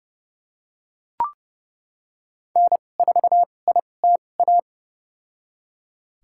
Morse code, heard as a keyed tone, about 30 wpm, spaced as for 22 wpm, 700 Hz. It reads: N4ITA
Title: USA Call Signs - 5 Characters - 2x Character Spacing - 30wpm